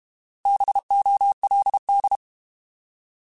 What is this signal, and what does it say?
Morse code, heard as a keyed tone, about 32 wpm, 800 Hz. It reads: BOLD
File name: Alphabetical / Alphabetical mp3s / Bs @30